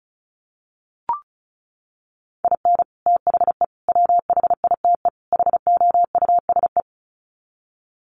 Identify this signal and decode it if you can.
Morse code, heard as a keyed tone, about 34 wpm, 700 Hz. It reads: IN THE WHITE HOUSE
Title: Sets of 4 Words 35wpm